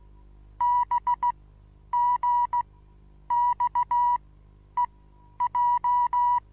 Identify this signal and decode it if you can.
Morse code, heard as a keyed tone, about 16 wpm, spaced as for 9 wpm, 970 Hz. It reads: BGXEJ
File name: Onlinelehrgang / morsen / DJ4UF Morsen25